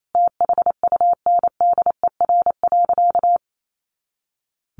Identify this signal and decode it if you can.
Morse code, heard as a keyed tone, about 28 wpm, 700 Hz. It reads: THUNDER.
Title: A Sound Of Thunder 28wpm